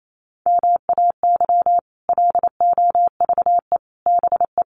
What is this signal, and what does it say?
Morse code, heard as a keyed tone, about 28 wpm, 700 Hz. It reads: MAY LOVE BE
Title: Sentences from Top 1400 Words 28wpm